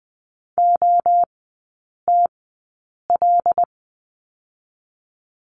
Morse code, heard as a keyed tone, about 20 wpm, spaced as for 8 wpm, 700 Hz. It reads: OTL
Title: 107 Lesson 18e - Letter m - Three Characters 20wpm